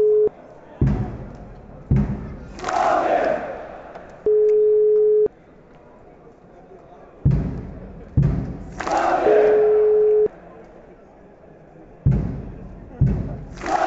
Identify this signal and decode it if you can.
Ringback tone (Europe, 425 Hz), with speech.